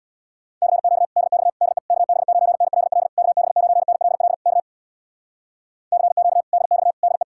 Morse code, heard as a keyed tone, about 67 wpm, 700 Hz. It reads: CQ DX DE KB9NLL KB9NLL K CQ DX DE